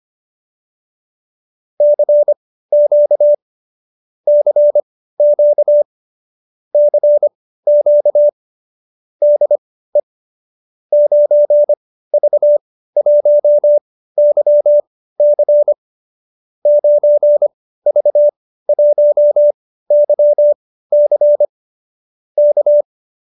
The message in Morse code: CQ CQ CQ DE 9V1YC 9V1YC K

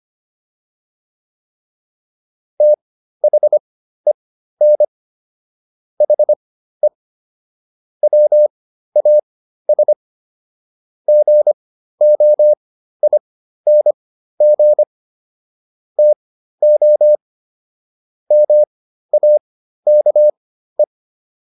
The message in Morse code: THEN HE WAS GOING TO MAKE

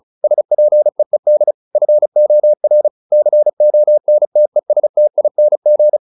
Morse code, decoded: SPEED FOR CONTESTING

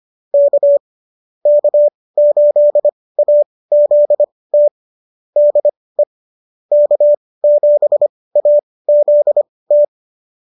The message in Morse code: K K8AZT DE K7AZT

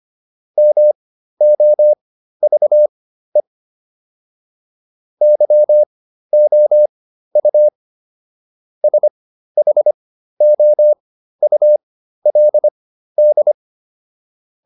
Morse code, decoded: MOVE YOU SHOULD

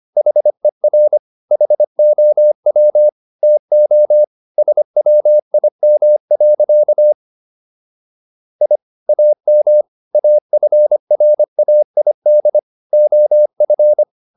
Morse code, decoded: HER HOW TO SWIM. I AM AFRAID OF

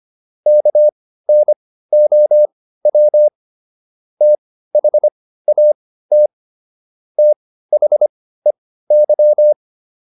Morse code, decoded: KNOW THAT THEY

